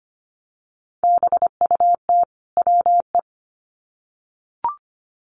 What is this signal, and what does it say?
Morse code, heard as a keyed tone, about 25 wpm, 700 Hz. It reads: BUT WE
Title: Sets of 2 Words 25wpm